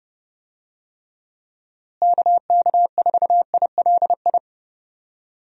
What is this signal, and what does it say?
Morse code, heard as a keyed tone, about 30 wpm, 700 Hz. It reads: KK4ILI
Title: USA Call Signs - 6 Characters - Repeated 3x - 30wpm